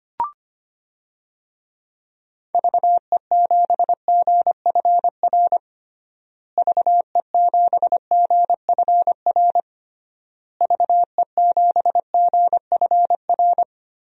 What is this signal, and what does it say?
Morse code, heard as a keyed tone, about 25 wpm, 700 Hz. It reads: VE7GFR VE7GFR VE7GFR
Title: DXCC-100 Call Signs - 6 Characters - Repeated 3x - 25wpm